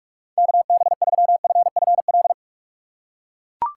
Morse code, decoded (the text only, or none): KB3FFL